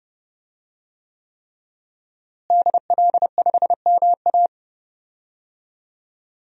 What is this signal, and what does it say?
Morse code, heard as a keyed tone, about 32 wpm, 700 Hz. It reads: DL5MA
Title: DXCC-100 Call Signs - 5 Characters - Repeated 3x - 30wpm